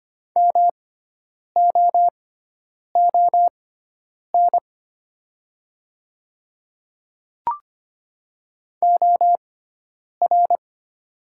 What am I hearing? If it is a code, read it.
Morse code, heard as a keyed tone, about 25 wpm, spaced as for 9 wpm, 700 Hz. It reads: MOON OR
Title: 6x Character Spacing - Top 500 Words - 25wpm